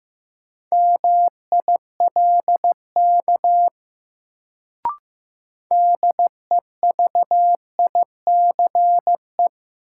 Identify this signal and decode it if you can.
Morse code, heard as a keyed tone, about 15 wpm, 700 Hz. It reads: MILK DEVICE
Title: Sentences from Top 1500 Words - New Word Drill - 15wpm